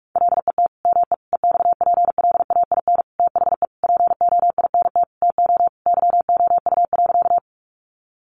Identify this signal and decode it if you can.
Morse code, heard as a keyed tone, about 44 wpm, 700 Hz. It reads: LET ME EXPLAIN THE POINT TO YOU.
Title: Sentences from Top 1100 Words 45wpm